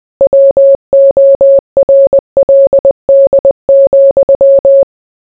Morse code, heard as a keyed tone, about 20 wpm, 550 Hz. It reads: WORLD,